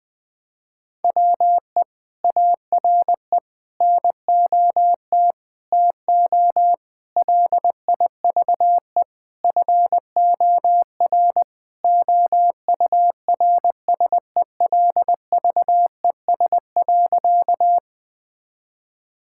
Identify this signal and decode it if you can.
Morse code, heard as a keyed tone, about 20 wpm, 700 Hz. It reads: WE ARE NOT TO LIVE FOR OURSELVES.
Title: Sentences from Top 1600 Words 20wpm